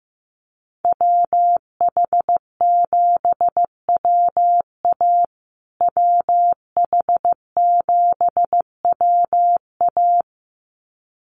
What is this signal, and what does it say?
Morse code, heard as a keyed tone, about 15 wpm, 700 Hz. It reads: WH7WA WH7WA
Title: USA Call Signs Repeated 2x 15wpm